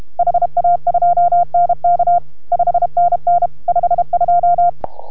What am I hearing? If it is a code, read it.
Morse code, heard as a keyed tone, about 31 wpm, 690 Hz. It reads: HA2NK 5NN 52